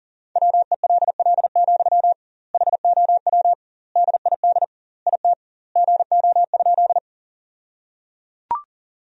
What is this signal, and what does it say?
Morse code, heard as a keyed tone, about 37 wpm, 700 Hz. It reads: WELL, HOW DID IT GO?
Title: 2x Word-Spacing - Sentences from Top 200 Words - 40wpm